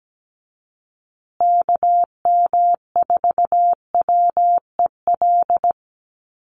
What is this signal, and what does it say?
Morse code, heard as a keyed tone, about 17 wpm, 700 Hz. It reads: KM4WEL
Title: USA Call Signs Repeated 2x 17wpm